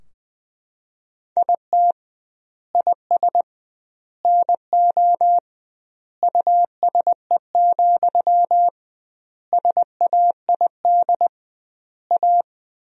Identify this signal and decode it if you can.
Morse code, heard as a keyed tone, about 19 wpm, 700 Hz. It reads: IT IS NO USE, SAID A
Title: 2x Word-Spacing - Sentences from Top 500 Words - 20wpm